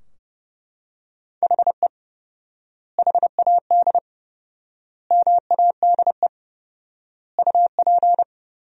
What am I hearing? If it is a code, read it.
Morse code, heard as a keyed tone, about 30 wpm, 700 Hz. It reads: HE HAD MADE UP
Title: 4x Word-Spacing - Sentences from Top 500 Words - 30wpm